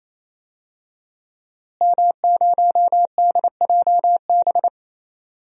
Morse code, heard as a keyed tone, about 28 wpm, 700 Hz. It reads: M0DJB
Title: DXCC-100 Call Signs - 5 Characters - Repeated 3x - 28wpm